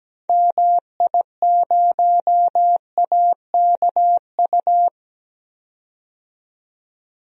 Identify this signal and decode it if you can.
Morse code, heard as a keyed tone, about 17 wpm, 700 Hz. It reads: MI0AKU
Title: DXCC-100 Call Signs - 6 Characters - 17wpm